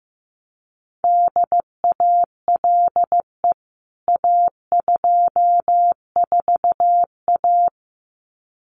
Morse code, heard as a keyed tone, about 14 wpm, 700 Hz. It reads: DALE A24A
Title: CWT Exchange 15wpm